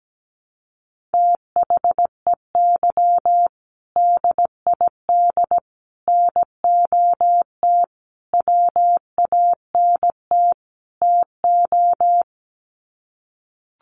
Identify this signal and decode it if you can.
Morse code, heard as a keyed tone, about 17 wpm, 700 Hz. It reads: THEY DID NOT WANT TO
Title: Sets of 5 Words 17wpm